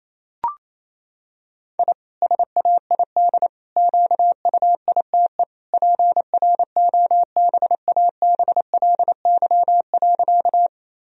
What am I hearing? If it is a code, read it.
Morse code, heard as a keyed tone, about 28 wpm, 700 Hz. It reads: I SAID QUITE PROBABLY.
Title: Sentences from Top 1000 Words - Review Part 4 - 28wpm